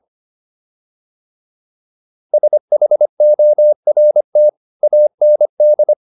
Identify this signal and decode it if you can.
Morse code, heard as a keyed tone, about 25 wpm, 600 Hz. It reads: SHORT AND